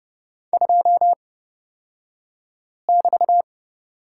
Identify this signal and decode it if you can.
Morse code, heard as a keyed tone, about 29 wpm, spaced as for 5 wpm, 690 Hz. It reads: 2<BT>
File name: Lesson 36d - BT - Two Characters 30wpm